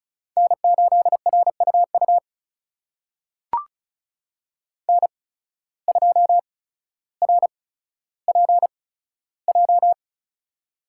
Morse code, heard as a keyed tone, about 36 wpm, spaced as for 15 wpm, 690 Hz. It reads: N8RUU N2RPJ
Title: USA Call Signs - 5 Characters - 8x Character Spacing - 35wpm